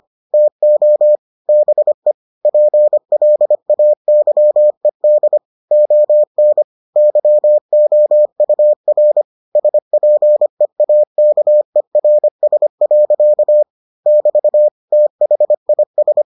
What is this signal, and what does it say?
Morse code, heard as a keyed tone, about 25 wpm, 600 Hz. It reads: TO BE PLAYED ON YOUR SPEAKERS. <BT> THIS